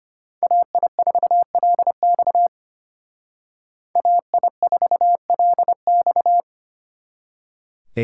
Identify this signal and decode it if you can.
Morse code, heard as a keyed tone, about 32 wpm, 700 Hz. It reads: AI4LX AI4LX